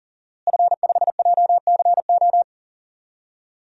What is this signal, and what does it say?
Morse code, heard as a keyed tone, about 40 wpm, 700 Hz. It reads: F5JCO
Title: DXCC-100 Call Signs - 5 Characters - Repeated 3x - 40wpm